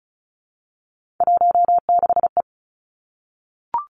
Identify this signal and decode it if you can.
Morse code, heard as a keyed tone, about 36 wpm, 700 Hz. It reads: 16E
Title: Three Challenging Letter-Number Combinations 35wpm